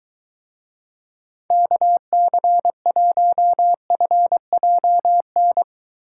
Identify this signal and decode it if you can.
Morse code, heard as a keyed tone, about 21 wpm, 700 Hz. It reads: KC1FJN